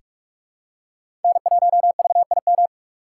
Morse code, heard as a keyed tone, about 48 wpm, 700 Hz. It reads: N1VIM
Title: USA Call Signs - 5 Characters - 6x Character Spacing - 45wpm